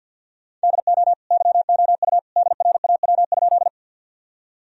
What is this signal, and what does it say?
Morse code, heard as a keyed tone, about 50 wpm, 700 Hz. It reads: DO YOU DRAW?
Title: Sentences from Top 1000 Words - Review Part 1 - 50wpm